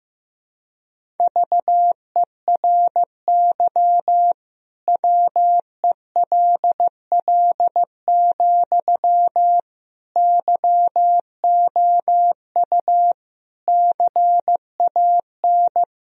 Morse code, heard as a keyed tone, about 15 wpm, 700 Hz. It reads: VERY WELL, YOU CAN
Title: Sentences from Top 1000 Words - Review Part 4 - 15wpm